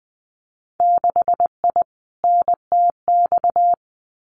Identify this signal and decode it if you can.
Morse code, heard as a keyed tone, about 20 wpm, 700 Hz. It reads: 6I NTX
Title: Winter Field Day Exchanges Encore 20wpm